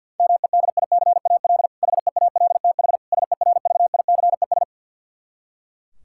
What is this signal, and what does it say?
Morse code, heard as a keyed tone, about 51 wpm, 700 Hz. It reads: MEDICAL HEALTH SERVICES
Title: K1USN SST QSO Practice 50wpm